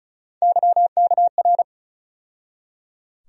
Morse code, heard as a keyed tone, about 36 wpm, 700 Hz. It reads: YKR
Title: USA Call Sign Postfixes - 3 Letters 35wpm